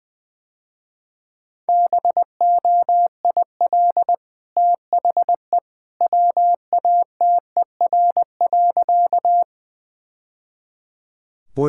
Morse code, heard as a keyed tone, about 20 wpm, 700 Hz. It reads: BOIL THE WATER.
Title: Easy Sentences with Hints - 20wpm